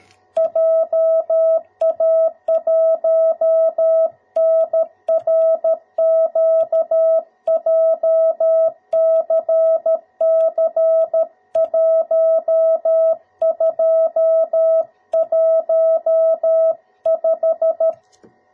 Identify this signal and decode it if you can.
Morse code, heard as a keyed tone, about 13 wpm, 650 Hz. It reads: JA1NRQ JCC1215
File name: jh8cbh / 2021cw / cw02b